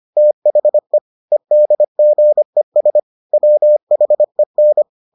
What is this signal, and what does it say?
Morse code, heard as a keyed tone, about 25 wpm, 600 Hz. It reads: THE EDGES WHEN